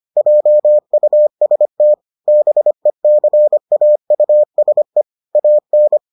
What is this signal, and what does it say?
Morse code, heard as a keyed tone, about 25 wpm, 600 Hz. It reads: JUST BECAUSE AN